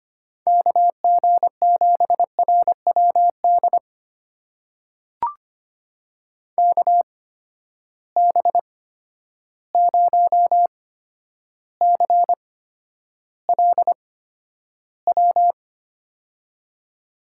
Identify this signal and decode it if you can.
Morse code, heard as a keyed tone, about 25 wpm, spaced as for 11 wpm, 700 Hz. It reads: KG7RWD KB0CLW
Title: USA Call Signs - 6 Characters - 8x Character Spacing - 25wpm